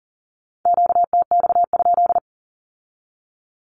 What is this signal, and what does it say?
Morse code, heard as a keyed tone, about 40 wpm, 700 Hz. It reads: QTX?
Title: Q-Codes with Spoken Meaning - 40wpm